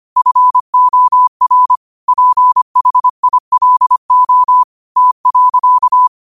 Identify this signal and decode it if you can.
Morse code, heard as a keyed tone, about 25 wpm, 1000 Hz. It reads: FOR PHILO T.
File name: qotdfiles / 25wpm